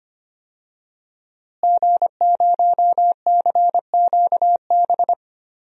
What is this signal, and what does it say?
Morse code, heard as a keyed tone, about 25 wpm, 700 Hz. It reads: G0CQB
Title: DXCC-100 Call Signs - 5 Characters - Repeated 2x - 25wpm